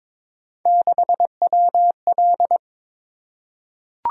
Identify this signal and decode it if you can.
Morse code, heard as a keyed tone, about 22 wpm, 700 Hz. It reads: 6WL
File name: practice / 22wpm / Three Letter-Number Combinations - 22wpm